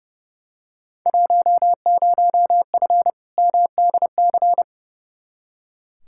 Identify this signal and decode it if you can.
Morse code, heard as a keyed tone, about 30 wpm, 700 Hz. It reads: 10F MDC
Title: Field Day Exchanges Encore - 30wpm